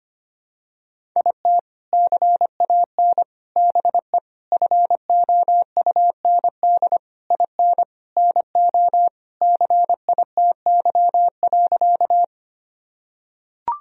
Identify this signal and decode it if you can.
Morse code, heard as a keyed tone, about 25 wpm, 700 Hz. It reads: IT CAN BE FOUND IN NO CITY.
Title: Sentences from Top 1000 Words - Review Part 1 - 25wpm